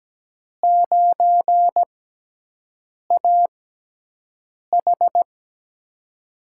Morse code, heard as a keyed tone, about 17 wpm, spaced as for 6 wpm, 700 Hz. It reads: 9AH
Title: USA Call Signs - 5 Characters - 6x Character Spacing - 17wpm